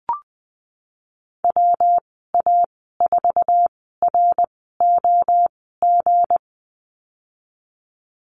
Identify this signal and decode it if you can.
Morse code, heard as a keyed tone, about 20 wpm, spaced as for 15 wpm, 700 Hz. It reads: WA4ROG
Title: 193 Lesson 31g - Letter v - Call Signs 20wpm